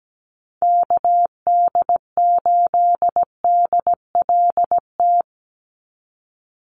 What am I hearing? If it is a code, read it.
Morse code, heard as a keyed tone, about 17 wpm, 700 Hz. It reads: KD8DLT
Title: USA Call Signs 17wpm